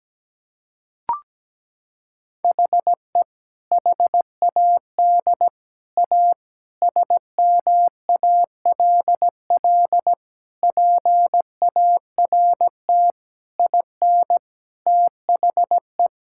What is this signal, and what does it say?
Morse code, heard as a keyed tone, about 17 wpm, 700 Hz. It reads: HE HAD A SMALL PART IN THE
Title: Sentences from Top 1000 Words - Review Part 5 - 17wpm